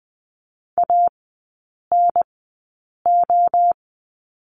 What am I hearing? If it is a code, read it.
Morse code, heard as a keyed tone, about 20 wpm, spaced as for 8 wpm, 700 Hz. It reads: ANO
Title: 047 Lesson 9e - Number 4 - Three Characters 20wpm